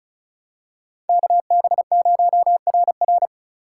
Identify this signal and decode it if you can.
Morse code, heard as a keyed tone, about 36 wpm, 700 Hz. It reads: KB0RR
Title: USA Call Signs - 2x Character Spacing - 35wpm